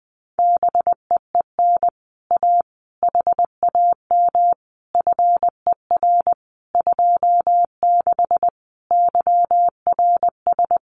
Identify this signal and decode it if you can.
Morse code, heard as a keyed tone, about 20 wpm, 700 Hz. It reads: BEEN A HAM FER 26 YRS
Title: Common QSO Phrases Encore 20wpm